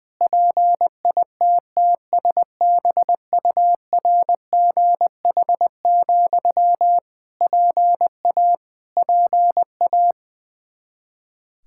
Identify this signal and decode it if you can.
Morse code, heard as a keyed tone, about 20 wpm, 700 Hz. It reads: PITTSBURGH, PA PA
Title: Practice QSO - Exchange 1 - Beginning - 20wpm